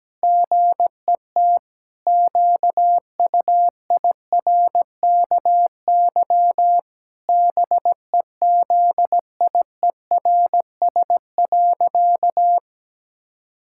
Morse code, heard as a keyed tone, about 17 wpm, 700 Hz. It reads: GET QUIRKY BEZIERS.
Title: Pangram Challenge v2 - 17wpm